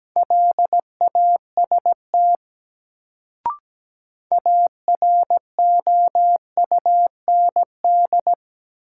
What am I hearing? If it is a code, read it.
Morse code, heard as a keyed tone, about 17 wpm, 700 Hz. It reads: LAST AROUND
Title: Sets of 2 Words - New Word Drill - 17wpm